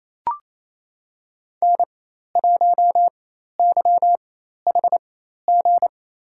Morse code, heard as a keyed tone, about 28 wpm, spaced as for 13 wpm, 700 Hz. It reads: N1YHG
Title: USA Call Signs - 5 Characters - 4x Character Spacing - 28wpm